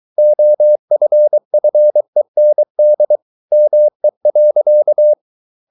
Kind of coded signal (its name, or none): Morse code